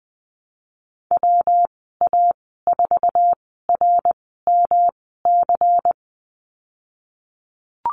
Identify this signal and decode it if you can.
Morse code, heard as a keyed tone, about 20 wpm, 700 Hz. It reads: W A 4 R M C
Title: 231 Lesson 37g - Letter x - Call Signs 20wpm